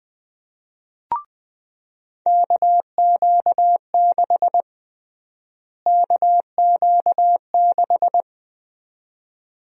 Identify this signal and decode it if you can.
Morse code, heard as a keyed tone, about 20 wpm, 700 Hz. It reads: KQ6 KQ6
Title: USA Call Sign Prefixes - Repeated 2x - 20wpm